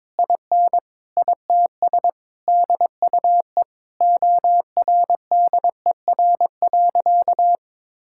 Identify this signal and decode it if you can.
Morse code, heard as a keyed tone, about 22 wpm, 700 Hz. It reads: IN ITS DUE ORDER.
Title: Sentences from Top 1000 Words - Review Part 6 - 22wpm